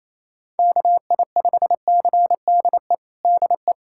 Morse code, heard as a keyed tone, about 28 wpm, 700 Hz. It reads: KI5CDE DE
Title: USA Callsign DE Callsign 28wpm